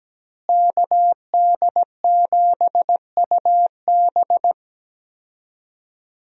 Morse code, heard as a keyed tone, about 17 wpm, 700 Hz. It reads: KD7UB
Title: USA Call Signs - 5 Characters - Repeated 2x - 17wpm